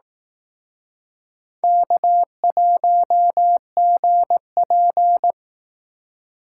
Morse code, heard as a keyed tone, about 18 wpm, 700 Hz. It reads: K1GP
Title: Speed Racing - USA 1x2 Call Signs 15wpm